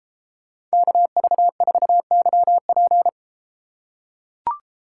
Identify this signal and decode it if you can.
Morse code, heard as a keyed tone, about 28 wpm, 700 Hz. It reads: KV4YP